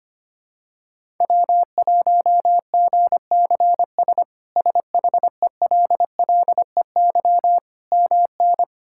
Morse code, heard as a keyed tone, about 25 wpm, 700 Hz. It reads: W1GCS SHELLEY MN
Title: K1USN SST QSO Practice - 25wpm